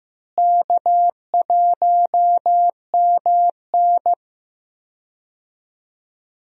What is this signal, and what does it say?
Morse code, heard as a keyed tone, about 16 wpm, 700 Hz. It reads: K1MN
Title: USA Call Signs - 4 Characters - Repeated 3x - 15wpm